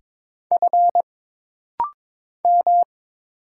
Morse code, heard as a keyed tone, about 22 wpm, 700 Hz. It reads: F M
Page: This rapid-fire timing is geared to push your ICR (Instant Character Recognition) skills.